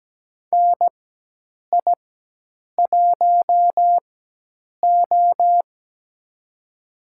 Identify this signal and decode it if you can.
Morse code, heard as a keyed tone, about 17 wpm, spaced as for 8 wpm, 700 Hz. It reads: NI1O
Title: USA Call Signs - 4 Characters - 4x Character Spacing - 17wpm